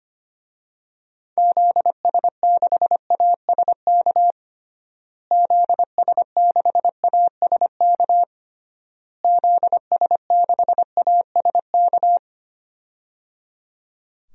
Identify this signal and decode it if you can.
Morse code, heard as a keyed tone, about 25 wpm, 700 Hz. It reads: ZS6ASK ZS6ASK ZS6ASK
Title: DXCC-100 Call Signs - 6 Characters - Repeated 3x - 25wpm